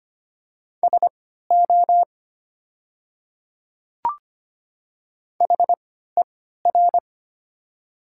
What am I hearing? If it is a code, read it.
Morse code, heard as a keyed tone, about 25 wpm, spaced as for 14 wpm, 700 Hz. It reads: SO HER
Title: 3x Character Spacing - Top 200 Words - 25wpm